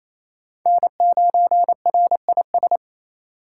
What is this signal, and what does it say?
Morse code, heard as a keyed tone, about 35 wpm, 700 Hz. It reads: N9RIS